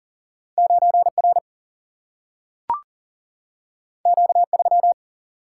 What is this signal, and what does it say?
Morse code, heard as a keyed tone, about 40 wpm, 700 Hz. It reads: 9R Q3
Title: Single-Letter Single-Number 40wpm